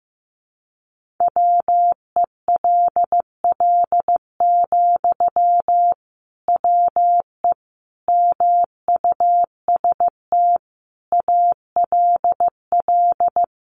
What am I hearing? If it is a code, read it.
Morse code, heard as a keyed tone, about 15 wpm, 700 Hz. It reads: WELL, WE MUST ALL
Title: Sentences from Top 1400 Words 15wpm